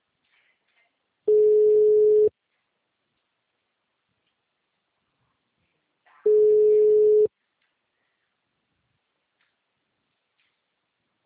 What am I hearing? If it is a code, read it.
Ringback tone (Europe, 425 Hz).
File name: out- -2001- - - .10808